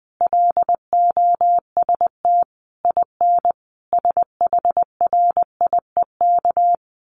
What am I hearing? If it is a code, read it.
Morse code, heard as a keyed tone, about 20 wpm, 700 Hz. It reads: LOST IN SHRIEK
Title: A Sound Of Thunder 20wpm